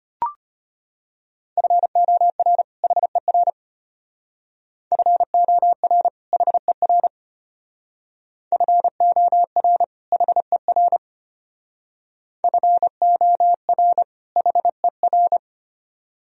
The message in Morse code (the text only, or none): FOR HER FOR HER FOR HER FOR HER